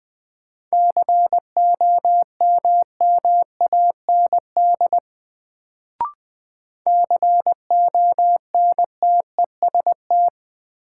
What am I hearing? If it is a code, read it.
Morse code, heard as a keyed tone, about 20 wpm, 700 Hz. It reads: COMMAND CONTEST
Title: Sentences from Top 1600 Words - New Word Drill Alt - 20wpm